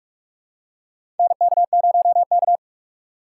Morse code, heard as a keyed tone, about 46 wpm, 700 Hz. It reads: NK0K
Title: USA Call Signs - 4x Character Spacing - 45wpm